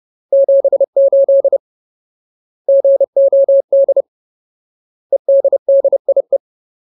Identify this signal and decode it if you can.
Morse code, heard as a keyed tone, about 30 wpm, 550 Hz. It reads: 78 GOD EDDIE
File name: Ultimate Word List / 30WPM Ultimate